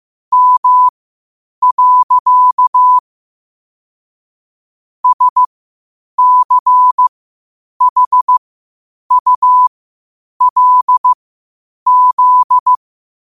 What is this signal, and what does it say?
Morse code, heard as a keyed tone, about 15 wpm, spaced as for 8 wpm, 1000 Hz. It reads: M. SCHULZ
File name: qotdfiles / 8wpm